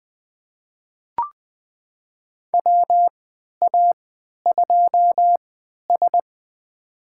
Morse code, heard as a keyed tone, about 20 wpm, spaced as for 11 wpm, 700 Hz. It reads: WA2S
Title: USA Call Signs - 4 Characters - 3x Character Spacing - 20wpm